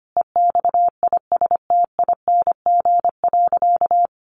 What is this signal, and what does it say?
Morse code, heard as a keyed tone, about 25 wpm, 700 Hz. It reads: EXISTING.
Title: Sentences from Top 1500 Words 25wpm